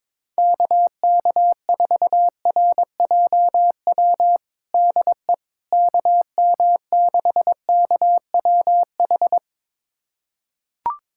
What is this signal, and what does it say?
Morse code, heard as a keyed tone, about 22 wpm, 700 Hz. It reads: KK4RJW DE KM6KWH
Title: USA Callsign DE Callsign 22wpm